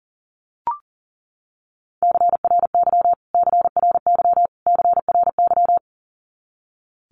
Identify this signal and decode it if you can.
Morse code, heard as a keyed tone, about 40 wpm, 700 Hz. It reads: CRY CRY CRY
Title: Top 500 Words Repeated 40wpm